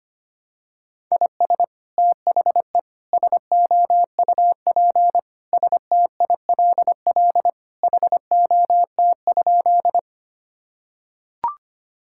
Morse code, heard as a keyed tone, about 25 wpm, 700 Hz. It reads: IS THE SOUP STILL HOT?
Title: Sentences from Top 1000 Words - Review Part 6 - 25wpm